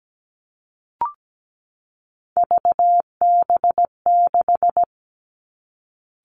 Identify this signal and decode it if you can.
Morse code, heard as a keyed tone, about 17 wpm, 700 Hz. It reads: VB6
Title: Three Challenging Letter-Number Combinations 17wpm